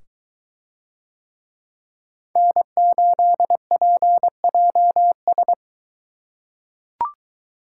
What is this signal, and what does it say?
Morse code, heard as a keyed tone, about 22 wpm, 700 Hz. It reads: N8PJS